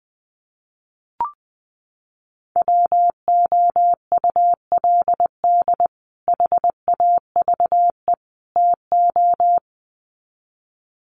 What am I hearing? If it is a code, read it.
Morse code, heard as a keyed tone, about 20 wpm, 700 Hz. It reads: WOULD HAVE TO
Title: Sets of 3 Words 20wpm